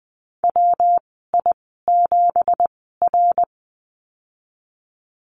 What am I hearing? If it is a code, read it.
Morse code, heard as a keyed tone, about 20 wpm, spaced as for 15 wpm, 700 Hz. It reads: WI7R
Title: 231 Lesson 37g - Letter x - Call Signs 20wpm